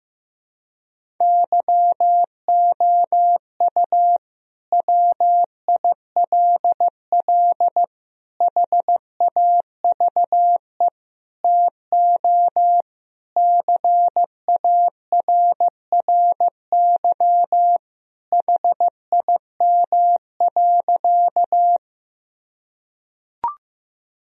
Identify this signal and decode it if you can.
Morse code, heard as a keyed tone, about 15 wpm, 700 Hz. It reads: YOU WILL HAVE TO CARRY HIM.
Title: Sentences from Top 1000 Words - Review Part 1 - 15wpm